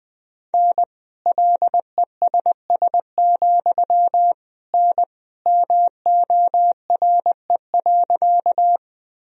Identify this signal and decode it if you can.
Morse code, heard as a keyed tone, about 20 wpm, 700 Hz. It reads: N LESS, N MORE.
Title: Sentences from Top 400 Words and Abbreviations 20wpm